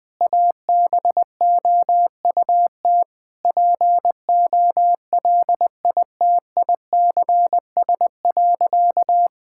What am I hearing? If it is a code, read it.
Morse code, heard as a keyed tone, about 21 wpm, 700 Hz. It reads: ABOUT POLITICS.
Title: Sentences from Top 1000 Words - Review Part 6 - 20wpm